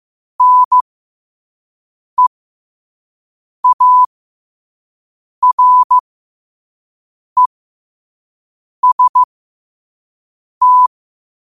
Morse code, heard as a keyed tone, about 15 wpm, spaced as for 5 wpm, 1000 Hz. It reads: NEAREST
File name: qotdfiles / 5wpm